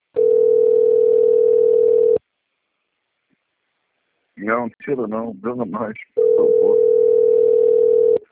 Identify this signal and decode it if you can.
Ringback tone (North America), 440 + 480 Hz, with speech.
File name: rg-1600-unknown- - - .24451